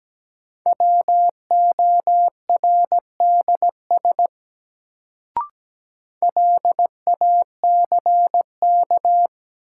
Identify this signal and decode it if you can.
Morse code, heard as a keyed tone, about 17 wpm, 700 Hz. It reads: WORDS LACK
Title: Sets of 3 Words - New Word Drill - 17wpm